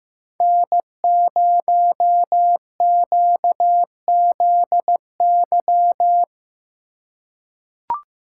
Morse code, heard as a keyed tone, about 15 wpm, 700 Hz. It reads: N0QZY
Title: USA Call Signs - 5 Characters - 2x Character Spacing - 15wpm